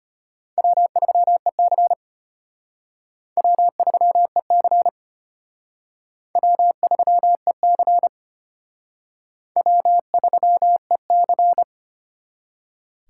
Morse code, heard as a keyed tone, about 31 wpm, 700 Hz. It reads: W3EC W3EC W3EC W3EC
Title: Speed Racing - USA 1x2 Call Signs 25wpm